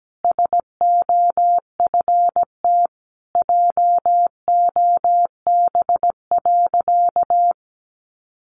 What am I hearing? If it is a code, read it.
Morse code, heard as a keyed tone, about 17 wpm, 700 Hz. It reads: SOFT JOB.
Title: Sentences from Top 1500 Words 17wpm